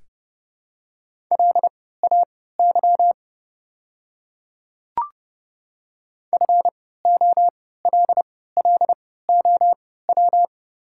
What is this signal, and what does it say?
Morse code, heard as a keyed tone, about 31 wpm, spaced as for 17 wpm, 700 Hz. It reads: LAY FOLLOW
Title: 3x Character Spacing - Top 500 Words - 30wpm